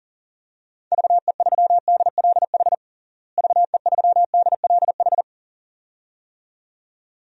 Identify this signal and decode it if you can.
Morse code, heard as a keyed tone, about 40 wpm, 700 Hz. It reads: VE3DLH VE3DLH
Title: DXCC-100 Call Signs - 6 Characters - Repeated 2x - 40wpm